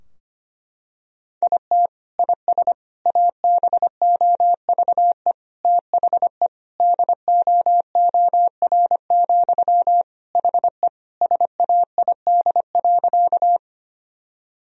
Morse code, heard as a keyed tone, about 25 wpm, 700 Hz. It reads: IT IS ABOVE THE DOOR, HE SAID.
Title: Sentences from Top 1000 Words - Review Part 2 - 25wpm